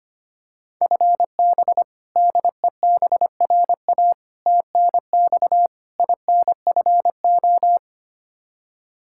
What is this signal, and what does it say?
Morse code, heard as a keyed tone, about 25 wpm, 700 Hz. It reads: FB DEBRA TNX INFO
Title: Practice QSO - Exchange 3 - More Info - 25wpm